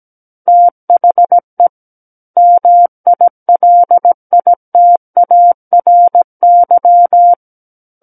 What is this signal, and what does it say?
Morse code, heard as a keyed tone, about 17 wpm, 700 Hz. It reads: THE MILITARY
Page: SKCC Learning Center - Morse Code Practice Files